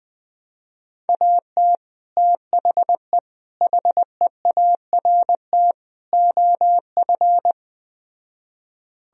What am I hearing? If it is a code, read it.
Morse code, heard as a keyed tone, about 20 wpm, 700 Hz. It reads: AT THE HEART OF